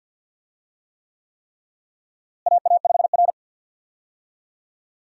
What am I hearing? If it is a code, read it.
Morse code, heard as a keyed tone, about 52 wpm, 700 Hz. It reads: AA5R
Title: USA Call Signs - 4 Characters - Repeated 3x - 50wpm